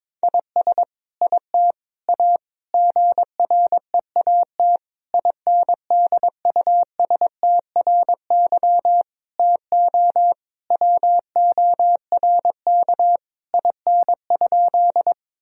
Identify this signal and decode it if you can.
Morse code, heard as a keyed tone, about 21 wpm, 700 Hz. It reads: IS IT A GREAT INDUSTRY TO WORK IN?
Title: Sentences from Top 1000 Words - Review Part 4 - 22wpm